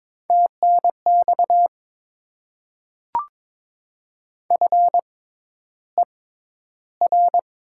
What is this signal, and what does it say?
Morse code, heard as a keyed tone, about 22 wpm, spaced as for 11 wpm, 700 Hz. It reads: TNX FER
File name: Top 20 QSO Elements - 6x Character Spacing - 22wpm